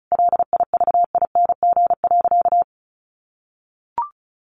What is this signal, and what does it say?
Morse code, heard as a keyed tone, about 34 wpm, 700 Hz. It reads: LIVING.
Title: Sentences from Top 1400 Words 35wpm